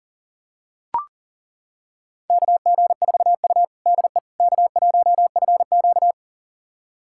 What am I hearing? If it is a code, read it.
Morse code, heard as a keyed tone, about 37 wpm, 700 Hz. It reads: KG4U DE K1FQ
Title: USA Callsign DE Callsign - 4 Characters - 40wpm